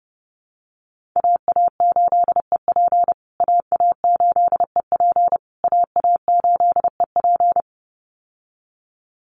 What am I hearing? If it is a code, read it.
Morse code, heard as a keyed tone, about 30 wpm, 700 Hz. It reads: AA8EP AA8EP AA8EP
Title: USA Call Signs Repeated 3x 30wpm